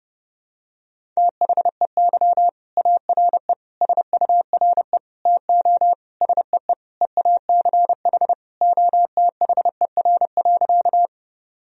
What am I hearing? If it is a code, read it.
Morse code, heard as a keyed tone, about 32 wpm, 700 Hz. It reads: THEY ARE SURE TO SEE EACH OTHER.
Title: Sentences from Top 1000 Words - Review Part 1 - 30wpm